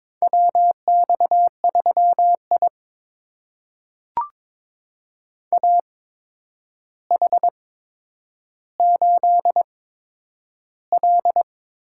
Morse code, heard as a keyed tone, about 22 wpm, spaced as for 9 wpm, 700 Hz. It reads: WX3I AH8L
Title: USA Call Signs - 8x Character Spacing - 22wpm